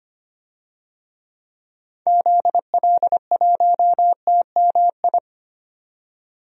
Morse code, heard as a keyed tone, about 25 wpm, 700 Hz. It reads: ZL1TMI
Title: DXCC-100 Call Signs - 6 Characters - 25wpm